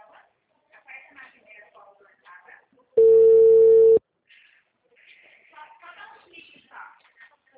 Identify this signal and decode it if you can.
Ringback tone (China, 450 Hz), with speech.